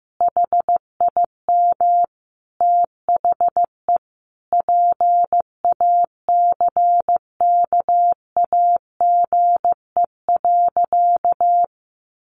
Morse code, heard as a keyed tone, about 15 wpm, 700 Hz. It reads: HIM THE PACKAGE.
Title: Easy Phrases 15wpm